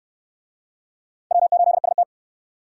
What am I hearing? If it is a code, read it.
Morse code, heard as a keyed tone, about 68 wpm, 700 Hz. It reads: W8ST